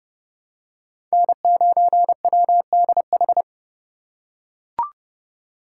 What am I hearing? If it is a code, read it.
Morse code, heard as a keyed tone, about 28 wpm, 700 Hz. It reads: N9WDH